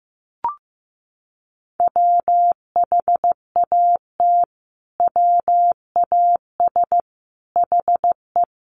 Morse code, heard as a keyed tone, about 15 wpm, 700 Hz. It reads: WHAT WAS HE
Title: Sentences from Top 100 Words 15wpm